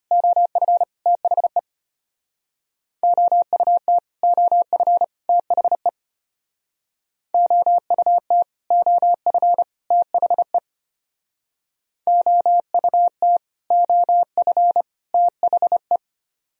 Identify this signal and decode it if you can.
Morse code, heard as a keyed tone, about 30 wpm, 700 Hz. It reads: OF THE OUT OF THE OUT OF THE OUT OF THE
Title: Speed-Racing - Sets of 3 Words 25wpm